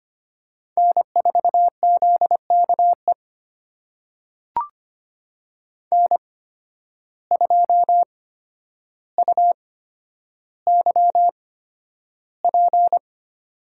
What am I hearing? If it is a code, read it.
Morse code, heard as a keyed tone, about 25 wpm, spaced as for 11 wpm, 700 Hz. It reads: N4ZKE N2UYP
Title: USA Call Signs - 5 Characters - 8x Character Spacing - 25wpm